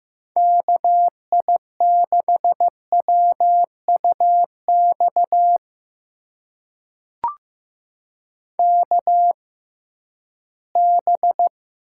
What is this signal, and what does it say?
Morse code, heard as a keyed tone, about 15 wpm, spaced as for 11 wpm, 700 Hz. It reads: KI6WUX KB
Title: USA Call Signs - 6 Characters - 6x Character Spacing - 15wpm